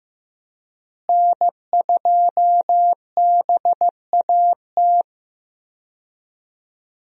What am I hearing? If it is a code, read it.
Morse code, heard as a keyed tone, about 15 wpm, 700 Hz. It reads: N2BAT
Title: USA Call Signs - 5 Characters - Repeated 2x - 15wpm